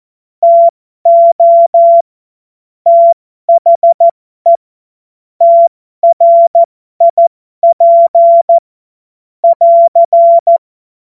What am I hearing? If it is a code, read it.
Morse code, heard as a keyed tone, about 14 wpm, spaced as for 12 wpm, 680 Hz. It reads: TO THE TRIP <AR>
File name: Morsekurs / Witze